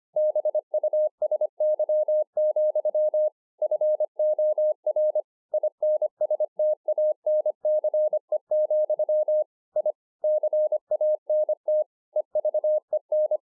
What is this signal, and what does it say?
Morse code, heard as a keyed tone, about 24 wpm, 610 Hz. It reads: BUSY, FOR INSTANCE, I CANT EVEN
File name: 2011-04-mp3 / THIS IS NOT MY SHACK 2 25WPM